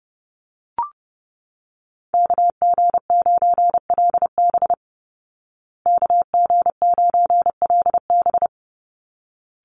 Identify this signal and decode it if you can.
Morse code, heard as a keyed tone, about 30 wpm, 700 Hz. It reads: KG9LB KG9LB
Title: Call Signs Repeated 3x - 30wpm